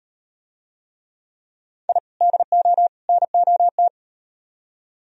Morse code, heard as a keyed tone, about 37 wpm, 700 Hz. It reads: I DO NOT